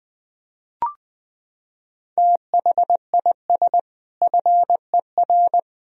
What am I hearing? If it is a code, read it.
Morse code, heard as a keyed tone, about 21 wpm, 700 Hz. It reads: THIS FER
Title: Sentences from Top 200 Words and Abbreviations 20wpm